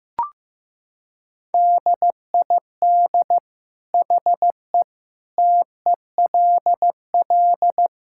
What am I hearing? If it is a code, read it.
Morse code, heard as a keyed tone, about 15 wpm, 700 Hz. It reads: DID HE TELL
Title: Sentences from Top 1600 Words 15wpm